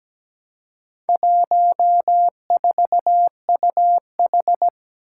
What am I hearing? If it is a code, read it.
Morse code, heard as a keyed tone, about 17 wpm, 700 Hz. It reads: 14UH
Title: Four Challenging Letter-Number Combinations 17wpm